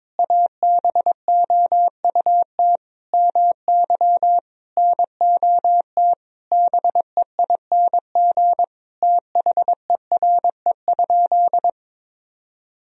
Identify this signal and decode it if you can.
Morse code, heard as a keyed tone, about 22 wpm, 700 Hz. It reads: ABOUT MY NOT BEING THERE?
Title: Sentences from Top 1000 Words - Review Part 3 - 22wpm